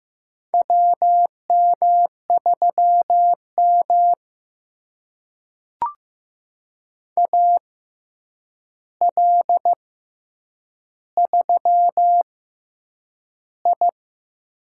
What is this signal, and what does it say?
Morse code, heard as a keyed tone, about 15 wpm, spaced as for 8 wpm, 700 Hz. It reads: WM3M AL3I
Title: USA Call Signs - 4 Characters - 6x Character Spacing - 15wpm